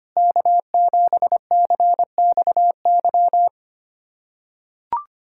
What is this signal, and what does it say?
Morse code, heard as a keyed tone, about 25 wpm, 700 Hz. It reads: K7CXY
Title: USA Callsign DE Callsign - 5 Characters - 25wpm